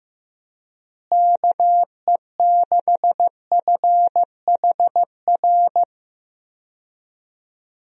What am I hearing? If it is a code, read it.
Morse code, heard as a keyed tone, about 15 wpm, 700 Hz. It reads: KE6FHR
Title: USA Call Signs - 6 Characters - Repeated 2x - 15wpm